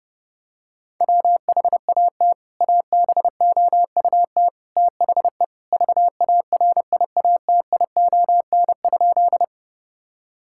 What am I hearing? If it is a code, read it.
Morse code, heard as a keyed tone, about 30 wpm, 700 Hz. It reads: WHAT ABOUT THE VARIATION?
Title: Sentences from Top 1600 Words 30wpm